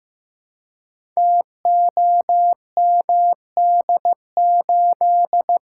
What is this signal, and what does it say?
Morse code, heard as a keyed tone, about 15 wpm, 700 Hz. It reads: TOMD8
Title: Code Groups - 5 Letter:Numbers - 15wpm